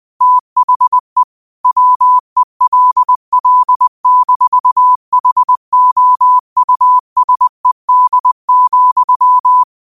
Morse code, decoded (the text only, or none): THE WELL-HOUSED,